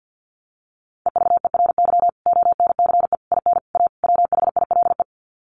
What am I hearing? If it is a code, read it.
Morse code, heard as a keyed tone, about 50 wpm, 700 Hz. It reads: EVERY ONCE IN A WHILE
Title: Sets of 5 Words 50wpm